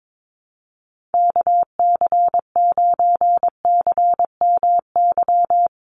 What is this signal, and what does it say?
Morse code, heard as a keyed tone, about 22 wpm, 700 Hz. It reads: KC9CMY
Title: USA Call Signs 22wpm